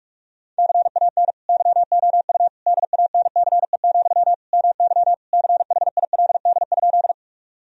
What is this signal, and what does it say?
Morse code, heard as a keyed tone, about 41 wpm, 700 Hz. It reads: CAN YOU DANCE, MY CHILD?
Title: Sentences from Top 1000 Words - Review Part 4 - 45wpm